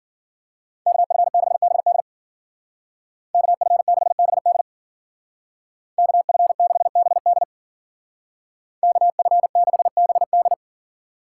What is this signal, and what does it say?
Morse code, heard as a keyed tone, about 53 wpm, 700 Hz. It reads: KF6BD KF6BD KF6BD KF6BD